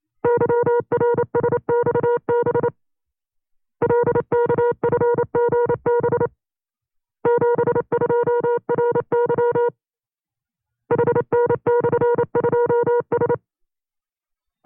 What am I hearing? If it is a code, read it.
Morse code, heard as a keyed tone, about 28 wpm, 480 Hz. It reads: YRSXB LKFGB 72RY HN/2S
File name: FreeFiles / CW-practice / 25WPM-side2